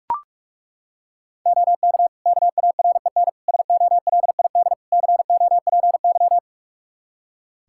Morse code, heard as a keyed tone, about 42 wpm, 700 Hz. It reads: OK KAREN SOLID COPY
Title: Practice QSO - Exchange 3 - More Info - 45wpm